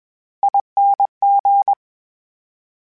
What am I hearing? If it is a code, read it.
Morse code, heard as a keyed tone, about 21 wpm, 800 Hz. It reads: ING
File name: Special INGs-mp3s / Special INGs-mp3s / VerySlowToFaster Ing @20